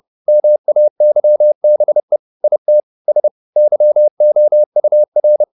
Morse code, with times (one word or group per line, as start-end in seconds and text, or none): 0.3-2.2 MAYBE
2.4-2.8 IT
3.1-3.3 S
3.6-5.4 YOUR